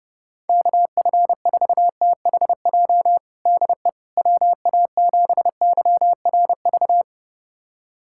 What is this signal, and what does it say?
Morse code, heard as a keyed tone, about 30 wpm, 700 Hz. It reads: KF4THJ DE WA7YRV
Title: USA Callsign DE Callsign 30wpm